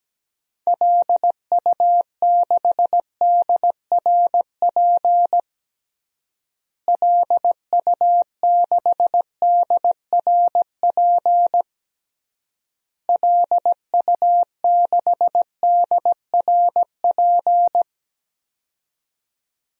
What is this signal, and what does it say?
Morse code, heard as a keyed tone, about 17 wpm, 700 Hz. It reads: LU6DRP LU6DRP LU6DRP
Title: DXCC-100 Call Signs - 6 Characters - Repeated 3x - 17wpm